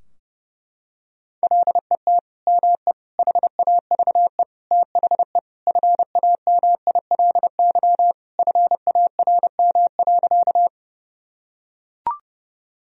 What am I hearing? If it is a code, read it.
Morse code, heard as a keyed tone, about 30 wpm, 700 Hz. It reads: LET ME HAVE THE FAMILY FARM.
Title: Sentences from Top 1000 Words - Review Part 2 - 30wpm